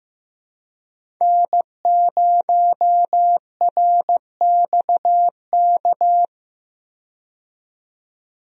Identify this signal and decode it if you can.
Morse code, heard as a keyed tone, about 15 wpm, 700 Hz. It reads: N0RXK
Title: USA Call Signs - 5 Characters - 15wpm